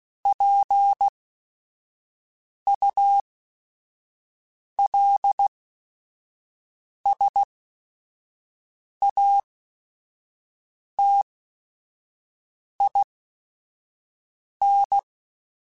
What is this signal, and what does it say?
Morse code, heard as a keyed tone, about 16 wpm, spaced as for 5 wpm, 800 Hz. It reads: PULSATIN
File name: Technical Subjects / Technical Subjects-mp3s / Sw Pwr Sup @5